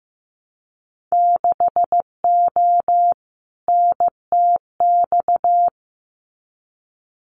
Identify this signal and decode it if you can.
Morse code, heard as a keyed tone, about 15 wpm, 700 Hz. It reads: 6O NTX
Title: Winter Field Day Exchanges Encore 15wpm